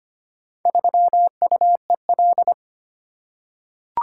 Morse code, heard as a keyed tone, about 25 wpm, 700 Hz. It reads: 3UEL
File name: practice / 25wpm / Four Letter-Number Combinations - 25wpm